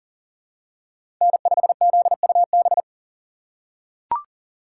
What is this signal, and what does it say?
Morse code, heard as a keyed tone, about 37 wpm, 700 Hz. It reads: N5ZUB
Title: USA Call Signs - 5 Characters - 4x Character Spacing - 40wpm